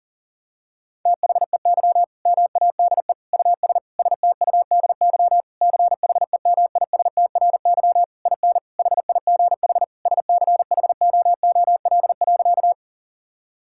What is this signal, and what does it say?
Morse code, heard as a keyed tone, about 37 wpm, 700 Hz. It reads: THEY MADE US STUDY CHEMISTRY IN HIGH SCHOOL.
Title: Sentences from Top 1000 Words - Review Part 6 - 40wpm